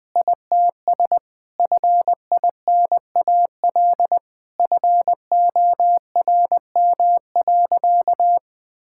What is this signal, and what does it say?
Morse code, heard as a keyed tone, about 20 wpm, 700 Hz. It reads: ITS FINAL FORM.
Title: Sentences with 7-Word Phrases - Spoken Beginning - 20wpm